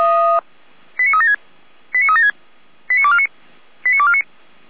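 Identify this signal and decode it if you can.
Five-tone selective call (ZVEI), heard as quick runs of five tones, tones of about 80 ms. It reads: 89376 89376 89249 89249
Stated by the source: ZVEI five-tone selective call 89171 received on SDR.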